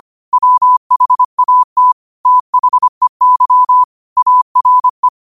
Morse code, heard as a keyed tone, about 25 wpm, 1000 Hz. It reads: WHAT THEY ARE
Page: New quotes every day in morse code at 25 Words per minute.